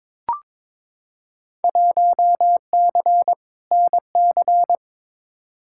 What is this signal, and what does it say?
Morse code, heard as a keyed tone, about 22 wpm, 700 Hz. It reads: 1C NC
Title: Field Day Exchanges - 22wpm